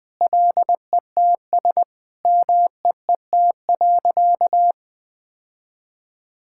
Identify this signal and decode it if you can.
Morse code, heard as a keyed tone, about 21 wpm, 700 Hz. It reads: LETS MEET.
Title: Sentences with 3-Word Phrases - 20wpm